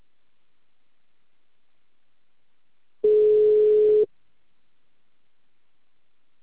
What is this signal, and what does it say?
Ringback tone (Europe, 425 Hz).